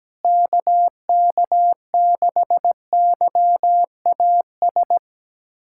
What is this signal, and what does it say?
Morse code, heard as a keyed tone, about 17 wpm, 700 Hz. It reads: KK6YAS
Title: USA Callsign DE Callsign - 6 Characters - 17wpm